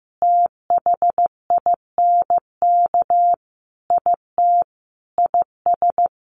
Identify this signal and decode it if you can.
Morse code, heard as a keyed tone, about 15 wpm, 700 Hz. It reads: THINK IT IS
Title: Sentences from Top 1400 Words 15wpm